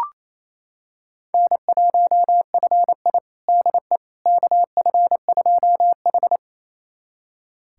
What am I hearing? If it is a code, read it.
Morse code, heard as a keyed tone, about 28 wpm, 700 Hz. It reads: N1FI DE KF2H
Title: USA Callsign DE Callsign - 4 Characters - 28wpm